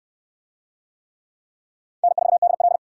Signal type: Morse code

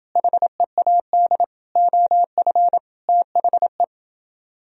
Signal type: Morse code

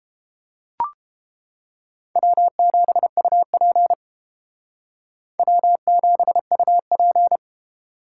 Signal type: Morse code

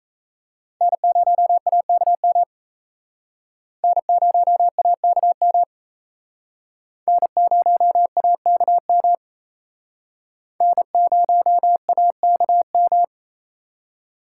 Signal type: Morse code

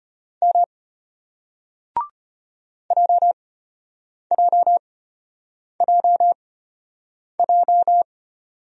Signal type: Morse code

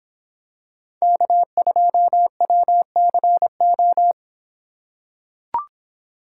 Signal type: Morse code

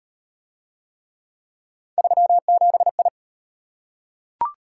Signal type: Morse code

